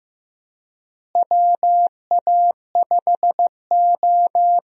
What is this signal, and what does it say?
Morse code, heard as a keyed tone, about 15 wpm, 700 Hz. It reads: WA5O
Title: USA Call Signs - 4 Characters - Repeated 2x - 15wpm